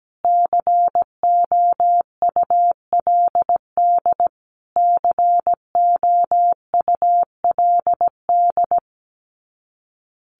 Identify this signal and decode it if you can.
Morse code, heard as a keyed tone, about 17 wpm, 700 Hz. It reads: COULD COULD
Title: Top 500 Words Repeated 17wpm